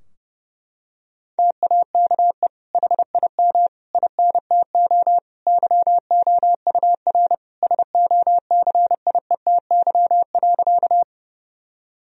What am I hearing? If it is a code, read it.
Morse code, heard as a keyed tone, about 30 wpm, 700 Hz. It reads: TAKE HIM INTO YOUR SOCIETY.
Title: Sentences from Top 1000 Words - Review Part 4 - 30wpm